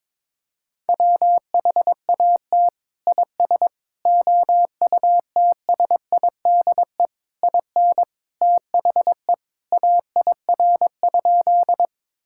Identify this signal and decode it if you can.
Morse code, heard as a keyed tone, about 22 wpm, 700 Hz. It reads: WHAT IS OUTSIDE IN THE AIR?
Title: Sentences from Top 1000 Words - Review Part 6 - 22wpm